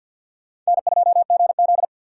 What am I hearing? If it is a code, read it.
Morse code, heard as a keyed tone, about 52 wpm, 700 Hz. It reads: N2GZ
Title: USA Call Signs - 4 Characters - 6x Character Spacing - 50wpm